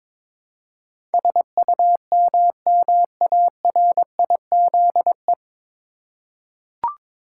Morse code, heard as a keyed tone, about 22 wpm, 700 Hz. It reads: SUMMARIZE
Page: Morse Code Ninja practice 22wpm